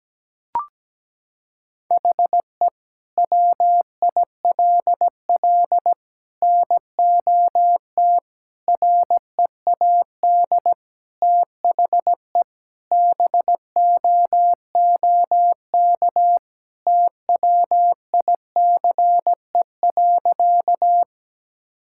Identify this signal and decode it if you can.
Morse code, heard as a keyed tone, about 17 wpm, 700 Hz. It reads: HE WILL NOT READ THE BOOK TWICE.
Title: Sentences from Top 1600 Words 17wpm